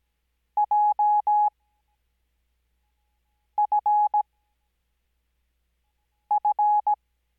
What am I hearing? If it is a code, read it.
Morse code, heard as a keyed tone, about 17 wpm, spaced as for 4 wpm, 840 Hz. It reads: JFF
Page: Farnsworth Spacing: 15WPM Character speed Spaced at 5WPM.